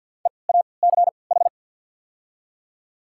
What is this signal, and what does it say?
Morse code, heard as a keyed tone, about 53 wpm, spaced as for 29 wpm, 690 Hz. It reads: EACH
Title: 3x Character Spacing - Top 100 Words - 50wpm